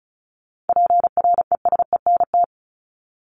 Morse code, heard as a keyed tone, about 34 wpm, 700 Hz. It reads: PRESENT
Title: ED Suffix - Top 100 Words with Base Words 35wpm